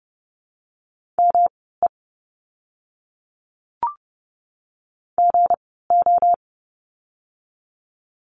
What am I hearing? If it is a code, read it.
Morse code, heard as a keyed tone, about 30 wpm, spaced as for 17 wpm, 700 Hz. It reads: ME GO
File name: course / 30:10wpm / 237 Lesson 38f - Letter q - Words 30wpm